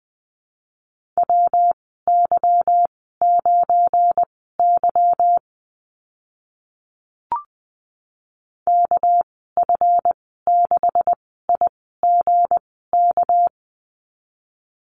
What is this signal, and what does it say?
Morse code, heard as a keyed tone, about 19 wpm, spaced as for 14 wpm, 700 Hz. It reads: WY9Y KF6IGK
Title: 200 Lesson 32g - Letter k - Call Signs 20wpm